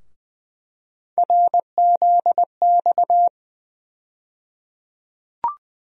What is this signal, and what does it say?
Morse code, heard as a keyed tone, about 20 wpm, 700 Hz. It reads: RZX
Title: USA Call Sign Postfixes - 3 Letters 20wpm